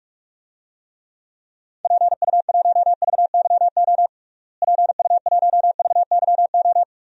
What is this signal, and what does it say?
Morse code, heard as a keyed tone, about 42 wpm, 700 Hz. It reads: PU1VYO PU1VYO
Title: DXCC-100 Call Signs - 6 Characters - Repeated 2x - 45wpm